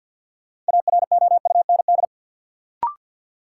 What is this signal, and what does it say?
Morse code, heard as a keyed tone, about 52 wpm, 700 Hz. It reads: AROUND
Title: Sets of 2 Words - New Word Drill - 50wpm